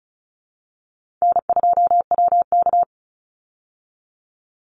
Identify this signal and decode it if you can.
Morse code, heard as a keyed tone, about 34 wpm, 700 Hz. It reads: N2WK
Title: Easy Call Signs Repeated 3x - 35wpm